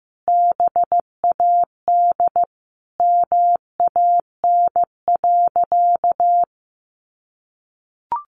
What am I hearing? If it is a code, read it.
Morse code, heard as a keyed tone, about 15 wpm, 700 Hz. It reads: BAD MAN.
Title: Sentences from Top 1200 Words 15wpm